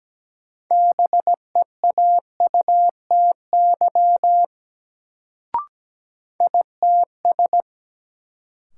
Morse code, heard as a keyed tone, about 17 wpm, 700 Hz. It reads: BEAUTY ITS
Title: Sentences from Top 600 Words - New Word Drill - 17wpm